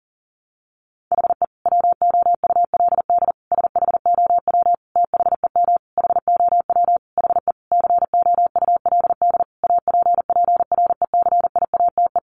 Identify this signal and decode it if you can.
Morse code, heard as a keyed tone, about 40 wpm, 700 Hz. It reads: HE WOULD SHOW THEM HOW HE COULD APPRECIATE
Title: Sentences from Top 1300 Words 40wpm